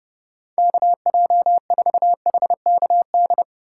Morse code, heard as a keyed tone, about 38 wpm, 700 Hz. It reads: KJ4HKD
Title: Speed Racing - USA 2x3 Call Signs 30wpm